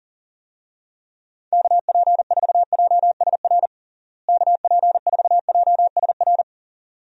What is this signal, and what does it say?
Morse code, heard as a keyed tone, about 40 wpm, 700 Hz. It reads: KP4JSR KP4JSR
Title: USA Call Signs - 6 Characters - Repeated 2x - 40wpm